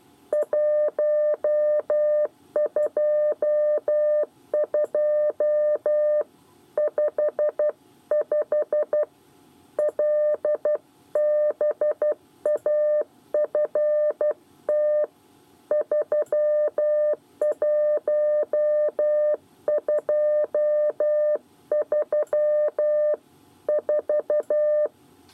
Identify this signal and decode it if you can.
Morse code, heard as a keyed tone, about 12 wpm, 600 Hz. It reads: 12255 LBAFT 31234
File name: 8 bis lezione cw 1